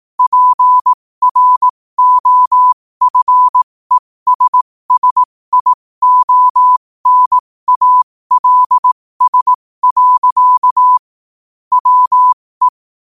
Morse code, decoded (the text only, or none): PROFESSIONALS. WE